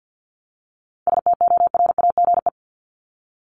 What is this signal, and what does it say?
Morse code, heard as a keyed tone, about 50 wpm, 700 Hz. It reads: STORAGE
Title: Top 1251-1500 Words 50wpm